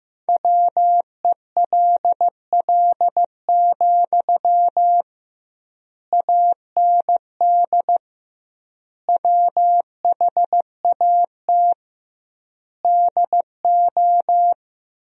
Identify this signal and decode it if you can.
Morse code, heard as a keyed tone, about 15 wpm, 700 Hz. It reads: WELL, AND WHAT DO
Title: 2x Word-Spacing - Sentences from Top 200 Words - 15wpm